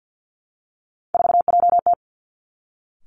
Morse code, heard as a keyed tone, about 50 wpm, 700 Hz. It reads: 4JT
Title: Three Challenging Letter-Number Combinations 50wpm